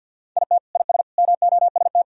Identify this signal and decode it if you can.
Morse code, heard as a keyed tone, about 47 wpm, 700 Hz. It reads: IT IS MOST
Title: Sentences from Top 1000 Words - Review Part 1 - 50wpm